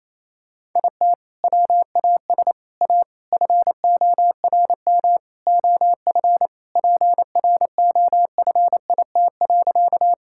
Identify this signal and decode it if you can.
Morse code, heard as a keyed tone, about 28 wpm, 700 Hz. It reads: IT WAS A FORM OF PROFIT.
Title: Sentences from Top 1000 Words - Review Part 5 - 28wpm